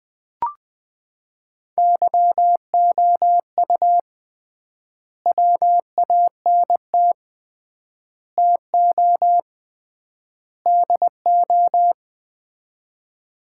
Morse code, heard as a keyed tone, about 20 wpm, 700 Hz. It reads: YOU WANT TO DO
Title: 3x Word-Spacing - Sets of 4 Words - 20wpm